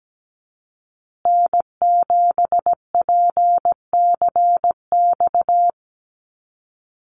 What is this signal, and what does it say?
Morse code, heard as a keyed tone, about 17 wpm, 700 Hz. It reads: N7PCX
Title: USA Call Signs Repeated 3x 17wpm